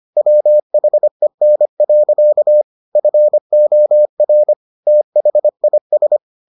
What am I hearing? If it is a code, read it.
Morse code, heard as a keyed tone, about 25 wpm, 600 Hz. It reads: WHEN. FOR THIS